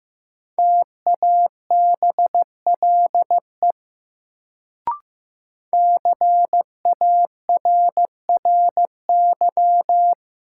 Morse code, heard as a keyed tone, about 15 wpm, 700 Hz. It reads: TABLE CARRY
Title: Sentences from Top 400 Words - New Word Drill - 15wpm